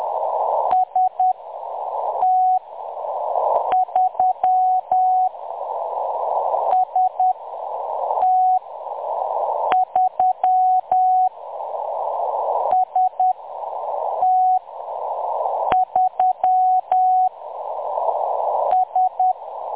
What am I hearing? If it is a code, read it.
Morse code, heard as a keyed tone, about 10 wpm, spaced as for 6 wpm, 740 Hz. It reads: ST3 ST3 ST3 S
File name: M23-ST3- -0858utc-10184khz